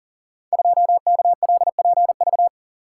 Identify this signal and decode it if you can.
Morse code, heard as a keyed tone, about 42 wpm, 700 Hz. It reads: 2KLPV
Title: Code Groups - 5 Letter:Numbers - 40wpm